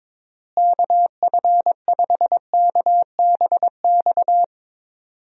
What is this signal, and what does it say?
Morse code, heard as a keyed tone, about 22 wpm, 700 Hz. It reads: KF5KBX
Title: USA Call Signs - 6 Characters - Repeated 2x - 22wpm